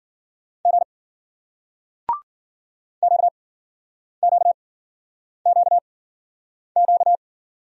Morse code, heard as a keyed tone, about 48 wpm, spaced as for 9 wpm, 700 Hz. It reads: G QQQQ
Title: Speed-Racing Single Letters 40wpm